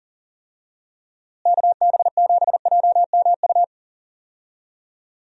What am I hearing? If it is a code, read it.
Morse code, heard as a keyed tone, about 40 wpm, 700 Hz. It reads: KB7JMU
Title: USA Call Signs - 6 Characters - Repeated 3x - 40wpm